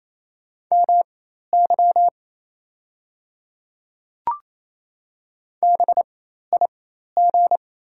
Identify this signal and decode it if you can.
Morse code, heard as a keyed tone, about 28 wpm, spaced as for 13 wpm, 700 Hz. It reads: MY BIG
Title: 4x Character Spacing - Top 300 Words - 28wpm